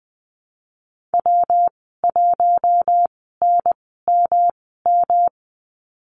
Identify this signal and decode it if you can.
Morse code, heard as a keyed tone, about 19 wpm, spaced as for 14 wpm, 700 Hz. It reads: W1NMM
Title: 161 Lesson 26g - Letter g - Call Signs 20wpm